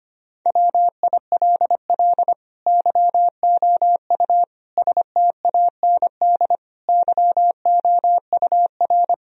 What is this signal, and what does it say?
Morse code, heard as a keyed tone, about 25 wpm, 700 Hz. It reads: WILL YOU STAND YOUR
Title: Sentences from Top 1000 Words - Review Part 2 - 25wpm